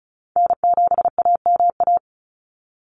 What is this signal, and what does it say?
Morse code, heard as a keyed tone, about 34 wpm, 700 Hz. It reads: N7AMA
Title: NA SOTA Chaser Callsigns 35wpm